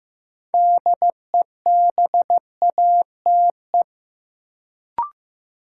Morse code, heard as a keyed tone, about 15 wpm, 700 Hz. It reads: DEBATE
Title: Sentences from Top 1100 Words - New Word Drill Alt - 15wpm